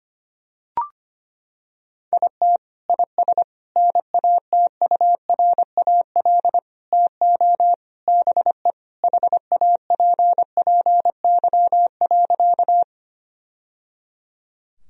Morse code, heard as a keyed tone, about 25 wpm, 700 Hz. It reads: IT IS NATURAL TO BE HAPPY.
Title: Sentences from Top 1000 Words - Review Part 4 - 25wpm